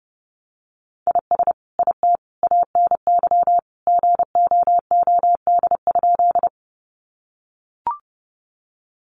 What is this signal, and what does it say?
Morse code, heard as a keyed tone, about 30 wpm, 700 Hz. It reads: IS IT ANY GOOD?
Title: Sentences from Top 200 Words 30wpm